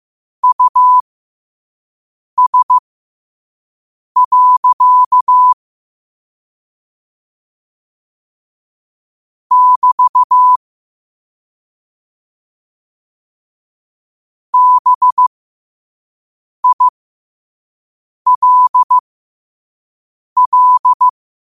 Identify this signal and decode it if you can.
Morse code, heard as a keyed tone, about 15 wpm, spaced as for 5 wpm, 1000 Hz. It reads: US. <BT> BILL
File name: qotdfiles / 5wpm